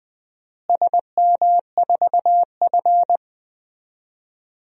Morse code, heard as a keyed tone, about 21 wpm, 700 Hz. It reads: SM4F
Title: DXCC-100 Call Signs - 4 Characters - Repeated 2x - 20wpm